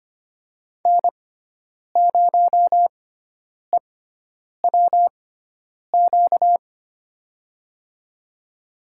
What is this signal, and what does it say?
Morse code, heard as a keyed tone, about 25 wpm, spaced as for 9 wpm, 700 Hz. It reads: N0EWQ
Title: USA Call Signs - 6x Character Spacing - 25wpm